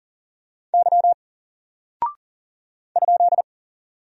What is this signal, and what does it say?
Morse code, heard as a keyed tone, about 40 wpm, 700 Hz. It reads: Y ?
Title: Single Character (Letter:Number:Punctuation:Prosign) Alt 40wpm